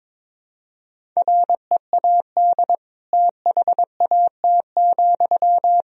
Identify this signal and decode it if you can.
Morse code, heard as a keyed tone, about 22 wpm, 700 Hz. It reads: READ THAT,
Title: Sentences from Top 200 Words and Abbreviations 22wpm